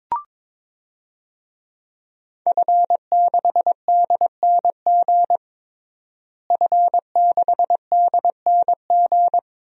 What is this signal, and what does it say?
Morse code, heard as a keyed tone, about 22 wpm, 700 Hz. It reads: F6DNG F6DNG
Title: DXCC-100 Call Signs - 5 Characters - Repeated 3x - 22wpm